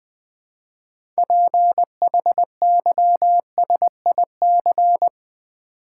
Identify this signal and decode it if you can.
Morse code, heard as a keyed tone, about 20 wpm, 700 Hz. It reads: PHYSIC
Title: Build A Word - 3 Letters At A Time - 8-Letter Words - 20wpm